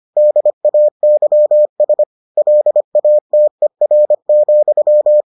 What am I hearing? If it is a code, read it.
Morse code, heard as a keyed tone, about 25 wpm, 600 Hz. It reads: DAYS LATER,